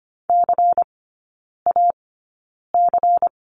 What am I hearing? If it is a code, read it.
Morse code, heard as a keyed tone, about 25 wpm, spaced as for 9 wpm, 700 Hz. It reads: CAC
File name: course / 25:10wpm / 093 Lesson 16e - Letter c - Three Characters 25wpm